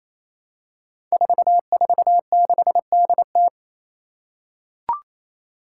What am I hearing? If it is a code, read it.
Morse code, heard as a keyed tone, about 28 wpm, 700 Hz. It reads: 446DT
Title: Five Challenging Letter-Number Combinations 28wpm